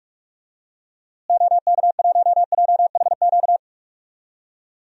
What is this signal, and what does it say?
Morse code, heard as a keyed tone, about 46 wpm, 700 Hz. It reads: OK1JHQ
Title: DXCC-100 Call Signs - 6 Characters - Repeated 2x - 45wpm